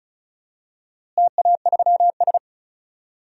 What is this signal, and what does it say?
Morse code, heard as a keyed tone, about 36 wpm, 700 Hz. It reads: TA3S
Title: DXCC-100 Call Signs - 4 Characters - Repeated 3x - 35wpm